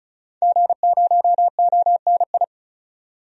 Morse code, heard as a keyed tone, about 36 wpm, 700 Hz. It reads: G0ONI
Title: DXCC-100 Call Signs - 5 Characters - 35wpm